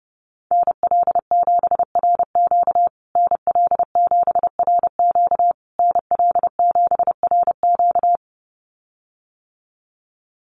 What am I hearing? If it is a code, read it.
Morse code, heard as a keyed tone, about 30 wpm, 700 Hz. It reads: NL7RQ NL7RQ NL7RQ
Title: USA Call Signs Repeated 3x 30wpm